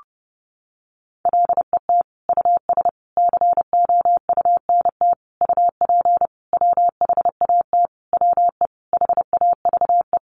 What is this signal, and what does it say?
Morse code, heard as a keyed tone, about 30 wpm, 700 Hz. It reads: LET US COUNT UP WHAT WE HAVE
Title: Sentences from Top 1500 Words 30wpm